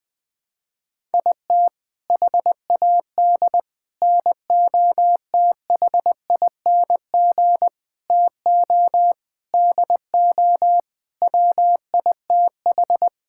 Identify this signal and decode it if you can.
Morse code, heard as a keyed tone, about 20 wpm, 700 Hz. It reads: IT HAD NOTHING TO DO WITH
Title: Sentences with 6-Word Phrases - 20wpm